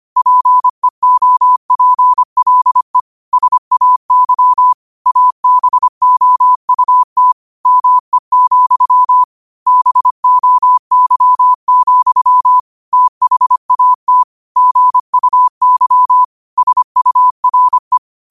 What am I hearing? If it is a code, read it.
Morse code, heard as a keyed tone, about 25 wpm, 1000 Hz. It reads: PEOPLE SAY ABOUT ME, BOY, THAT GUY SURE
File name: qotdfiles / 25wpm